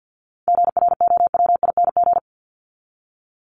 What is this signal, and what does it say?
Morse code, heard as a keyed tone, about 49 wpm, 700 Hz. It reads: GROWING
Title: ING Suffix - Top 100 Words with Base Words 50wpm